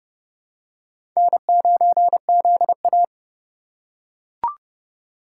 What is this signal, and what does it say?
Morse code, heard as a keyed tone, about 30 wpm, 700 Hz. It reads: N9ZA
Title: USA Call Signs - 3x Character Spacing - 30wpm